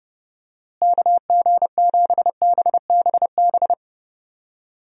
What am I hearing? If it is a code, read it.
Morse code, heard as a keyed tone, about 32 wpm, 700 Hz. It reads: KG7BBB
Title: USA Call Signs - 6 Characters - Repeated 2x - 30wpm